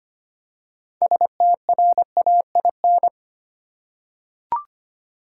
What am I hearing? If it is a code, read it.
Morse code, heard as a keyed tone, about 25 wpm, 700 Hz. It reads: STRAIN
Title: Six Letter Words - 25wpm